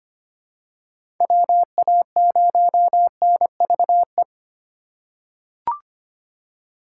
Morse code, heard as a keyed tone, about 25 wpm, 700 Hz. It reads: WA0NVE
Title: USA Call Signs - 6 Characters - 2x Character Spacing - 25wpm